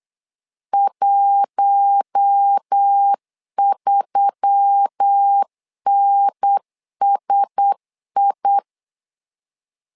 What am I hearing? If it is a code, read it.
Morse code, heard as a keyed tone, about 8 wpm, 800 Hz. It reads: 13NSI
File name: LEZIONI / 5 LBVC / 5 LBVC 8wpm 2935